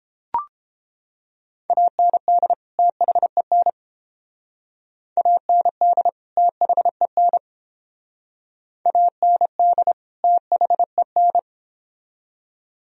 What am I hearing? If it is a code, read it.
Morse code, heard as a keyed tone, about 29 wpm, 700 Hz. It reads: AND THEN AND THEN AND THEN